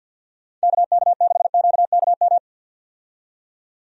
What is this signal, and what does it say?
Morse code, heard as a keyed tone, about 51 wpm, 700 Hz. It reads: KK6QKM
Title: USA Call Signs - 6 Characters - 50wpm